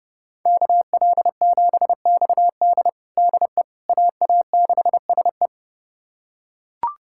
Morse code, heard as a keyed tone, about 32 wpm, 700 Hz. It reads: KL7XD DE AA6SE
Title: USA Callsign DE Callsign - 5 Characters - 30wpm